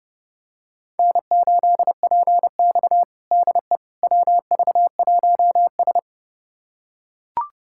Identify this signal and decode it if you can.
Morse code, heard as a keyed tone, about 31 wpm, 700 Hz. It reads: N8PX DE WV1S
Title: USA Callsign DE Callsign - 4 Characters - 30wpm